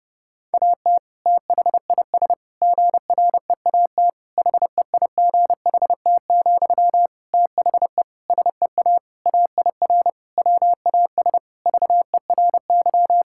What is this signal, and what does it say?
Morse code, heard as a keyed tone, about 30 wpm, 700 Hz. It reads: AT THIS GREAT HEIGHT, THE SEA AIR WAS VERY
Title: Sentences from Top 1600 Words 30wpm